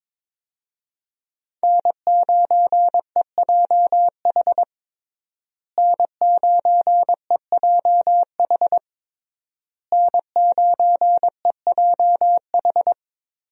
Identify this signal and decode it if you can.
Morse code, heard as a keyed tone, about 22 wpm, 700 Hz. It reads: N9EJH N9EJH N9EJH
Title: USA Call Signs - 5 Characters - Repeated 3x - 22wpm